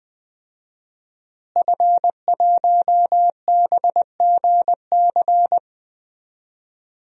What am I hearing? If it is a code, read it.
Morse code, heard as a keyed tone, about 20 wpm, 700 Hz. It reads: F1BGC
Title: DXCC-100 Call Signs - 5 Characters - Repeated 3x - 20wpm